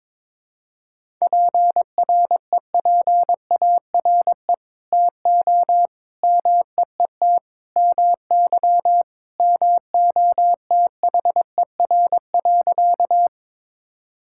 Morse code, heard as a keyed tone, about 21 wpm, 700 Hz. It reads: PREPARE TO MEET MY MOTHER.
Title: Sentences from Top 1000 Words - Review Part 6 - 22wpm